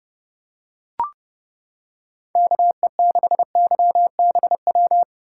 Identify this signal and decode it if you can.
Morse code, heard as a keyed tone, about 29 wpm, 700 Hz. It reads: KE6YBW